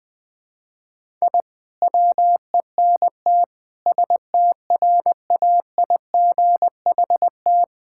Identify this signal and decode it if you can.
Morse code, heard as a keyed tone, about 20 wpm, 700 Hz. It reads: I WENT STRAIGHT
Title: Sentences from Top 1000 Words - Review Part 6 - 20wpm